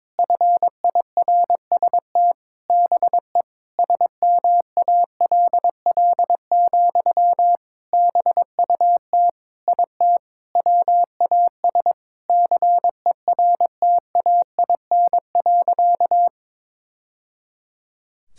Morse code, heard as a keyed tone, about 22 wpm, 700 Hz. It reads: FIRST BE SMALL, BUT IT WAS CERTAIN.
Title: Sentences from Top 1000 Words - Review Part 6 - 22wpm